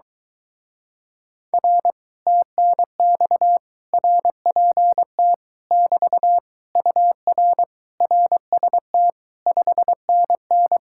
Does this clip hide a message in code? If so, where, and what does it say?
Yes: Morse code from 1.5 to 10.8 s, heard as a keyed tone, about 22 wpm, 700 Hz. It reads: R TNX RPT <BT> UR RST 5NN